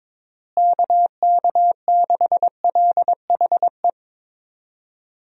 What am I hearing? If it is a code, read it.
Morse code, heard as a keyed tone, about 22 wpm, 700 Hz. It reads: KK6LHE
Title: USA Call Signs - 6 Characters - Repeated 2x - 22wpm